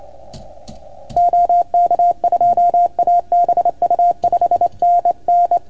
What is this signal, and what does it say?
Morse code, heard as a keyed tone, about 28 wpm, 690 Hz. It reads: OK2ABU 5NN
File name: Archiv / CUC310 / OK1FGD CUC 310 pileup